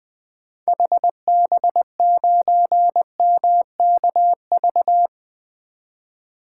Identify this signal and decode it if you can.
Morse code, heard as a keyed tone, about 20 wpm, 700 Hz. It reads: HB9MKV
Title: 2023 SOTA Chaser Call Signs - 20wpm